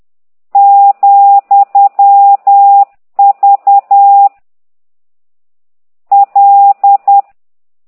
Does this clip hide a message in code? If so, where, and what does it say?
Yes: Morse code from 0.6 to 7.2 s, heard as a keyed tone, about 10 wpm, 800 Hz. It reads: , V L